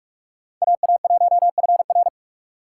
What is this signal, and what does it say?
Morse code, heard as a keyed tone, about 42 wpm, 700 Hz. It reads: AA1FR
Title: USA Call Signs - 5 Characters - 6x Character Spacing - 45wpm